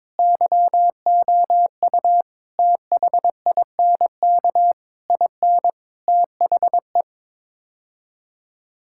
Morse code, heard as a keyed tone, about 22 wpm, 700 Hz. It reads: YOU THINK IN THE
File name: practice / 22wpm / Sets of 8 Words - 22wpm